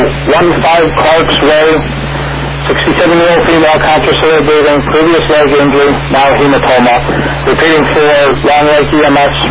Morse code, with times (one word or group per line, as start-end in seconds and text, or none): none